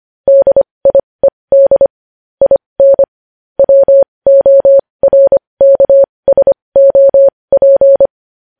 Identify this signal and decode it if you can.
Morse code, heard as a keyed tone, about 25 wpm, spaced as for 20 wpm, 550 Hz. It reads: DIED IN WORKSOP
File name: podcast / News headlines-1x-25x20